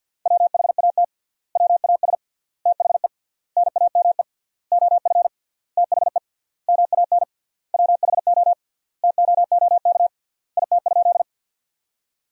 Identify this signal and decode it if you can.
Morse code, heard as a keyed tone, about 52 wpm, 700 Hz. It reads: WHAT WAS THE NAME OF THE MAN WHO TOOK IT?
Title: 3x Word-Spacing - Sentences from Top 300 Words - 50wpm